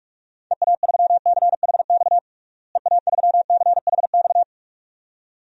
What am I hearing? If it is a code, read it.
Morse code, heard as a keyed tone, about 46 wpm, 700 Hz. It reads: EA3CHX EA3CHX
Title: DXCC-100 Call Signs - 6 Characters - Repeated 2x - 45wpm